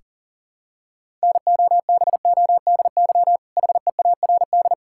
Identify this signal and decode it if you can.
Morse code, heard as a keyed tone, about 40 wpm, 700 Hz. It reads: NOBODY HEARD
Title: Sentences from Top 1600 Words 40wpm